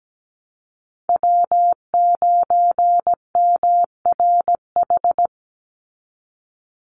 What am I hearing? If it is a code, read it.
Morse code, heard as a keyed tone, about 17 wpm, 700 Hz. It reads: W9MRH
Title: NA SOTA Chaser Callsigns 17wpm